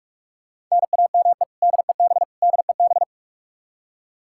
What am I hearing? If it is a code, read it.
Morse code, heard as a keyed tone, about 41 wpm, 700 Hz. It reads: NAME DEB DEB
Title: Practice QSO - Exchange 1 - Beginning - 45wpm